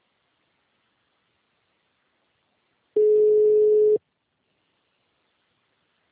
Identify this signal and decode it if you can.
Ringback tone (Europe, 425 Hz).
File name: out- -2001- - - .57